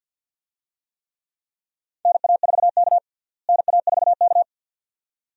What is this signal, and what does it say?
Morse code, heard as a keyed tone, about 51 wpm, 700 Hz. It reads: NA4K NA4K
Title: USA Call Signs - 4 Characters - Repeated 2x - 50wpm